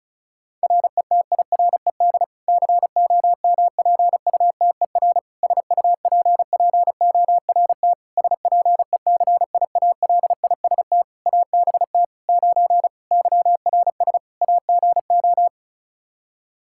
Morse code, heard as a keyed tone, about 36 wpm, 700 Hz. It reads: RETIRED COMPUTER SUPPORT SPECIALIST ABT 9 YRS AGO
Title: Practice QSO - Exchange 3 - More Info - 35wpm